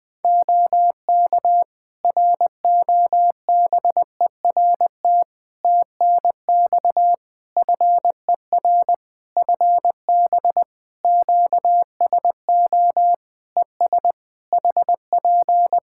Morse code, heard as a keyed tone, about 20 wpm, 700 Hz. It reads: OK ROBERT TNX FER FB QSO ES HP
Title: K1USN SST QSO Practice 20wpm